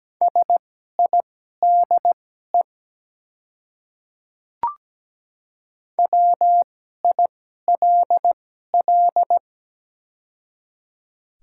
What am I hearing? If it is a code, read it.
Morse code, heard as a keyed tone, about 17 wpm, spaced as for 12 wpm, 690 Hz. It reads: SIDE WILL
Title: 2x Character Spacing - Top 300 Words - 17wpm